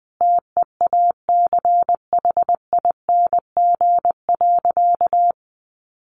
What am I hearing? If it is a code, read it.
Morse code, heard as a keyed tone, about 20 wpm, 700 Hz. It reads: TEACHING.
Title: Sentences from Top 1100 Words 20wpm